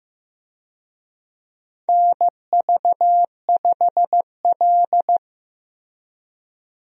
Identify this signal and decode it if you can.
Morse code, heard as a keyed tone, about 15 wpm, 700 Hz. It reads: NV5L
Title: USA Call Signs - 4 Characters - Repeated 3x - 15wpm